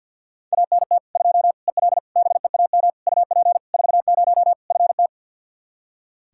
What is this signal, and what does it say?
Morse code, heard as a keyed tone, about 46 wpm, 700 Hz. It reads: ANT 2 EL BEAM UP 40 FT
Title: Practice QSO - Exchange 2 - Info - 50wpm